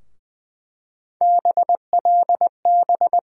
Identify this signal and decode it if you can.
Morse code, heard as a keyed tone, about 20 wpm, 700 Hz. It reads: BLB
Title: USA Call Sign Postfixes - 3 Letters 20wpm